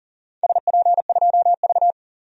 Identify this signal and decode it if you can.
Morse code, heard as a keyed tone, about 42 wpm, 700 Hz. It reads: SP2V
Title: DXCC-100 Call Signs - 4 Characters - 40wpm